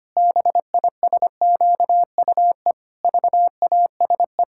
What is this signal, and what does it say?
Morse code, heard as a keyed tone, about 25 wpm, 700 Hz. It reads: BISQUE VASE
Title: Pangram Challenge v2 - 25wpm